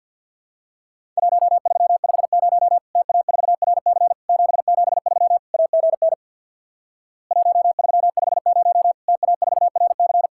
Morse code, heard as a keyed tone, about 50 wpm, 700 Hz. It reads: 1350 TA4RC 773 AGN 1350 TA4RC
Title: Worked All Europe DX Contest - QTC Exchange with Repeats - 50wpm